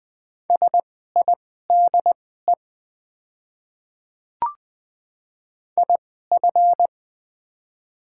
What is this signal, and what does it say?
Morse code, heard as a keyed tone, about 20 wpm, spaced as for 15 wpm, 700 Hz. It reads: SIDE IF
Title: 2x Character Spacing - Top 100 Words - 20wpm